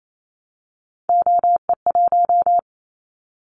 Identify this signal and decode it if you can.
Morse code, heard as a keyed tone, about 28 wpm, 700 Hz. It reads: OE1
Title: Callsign Prefixes for Top 25 DXCC - 28wpm